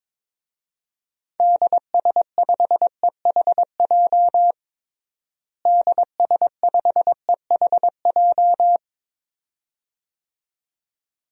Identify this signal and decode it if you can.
Morse code, heard as a keyed tone, about 22 wpm, 700 Hz. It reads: DS5EHJ DS5EHJ
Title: DXCC-100 Call Signs - 6 Characters - Repeated 2x - 22wpm